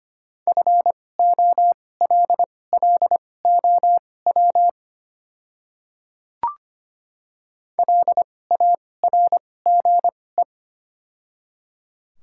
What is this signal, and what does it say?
Morse code, heard as a keyed tone, about 25 wpm, spaced as for 18 wpm, 700 Hz. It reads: FOLLOW LARGE
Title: 2x Character Spacing - Top 300 Words - 25wpm